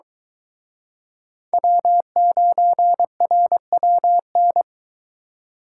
Morse code, heard as a keyed tone, about 23 wpm, 700 Hz. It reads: W9RWN